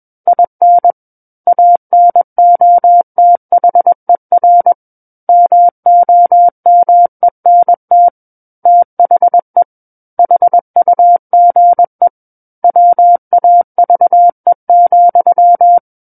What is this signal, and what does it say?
Morse code, heard as a keyed tone, about 21 wpm, 700 Hz. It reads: IN ANOTHER MOMENT THE HUGE WAVE,
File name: Book One / 12-Chapter 12 - 21 WPM